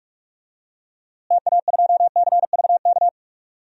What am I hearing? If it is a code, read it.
Morse code, heard as a keyed tone, about 42 wpm, 700 Hz. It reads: TA2CVK
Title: DXCC-100 Call Signs - 6 Characters - Repeated 3x - 45wpm